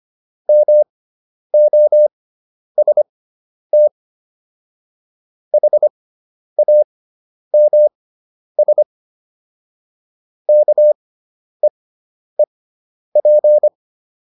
Morse code, heard as a keyed tone, about 25 wpm, spaced as for 10 wpm, 600 Hz. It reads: MOST HAMS KEEP